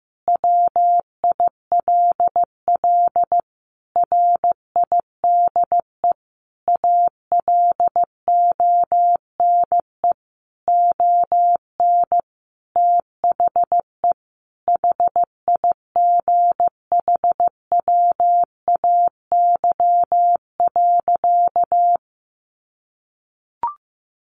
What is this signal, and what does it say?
Morse code, heard as a keyed tone, about 15 wpm, 700 Hz. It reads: WILL RIDE ALONE ON THE HIGHWAY.
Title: Sentences from Top 1400 Words 15wpm